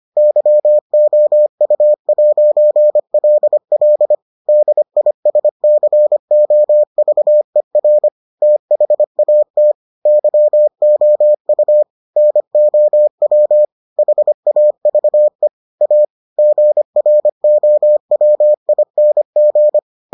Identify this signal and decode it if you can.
Morse code, heard as a keyed tone, about 25 wpm, 600 Hz. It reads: YOU'LL DISCOVER THAT YOU NOW HAVE A GROWING